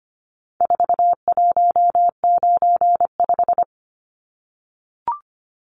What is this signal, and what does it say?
Morse code, heard as a keyed tone, about 25 wpm, 700 Hz. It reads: 4195
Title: Numbers - 4-Digit 25wpm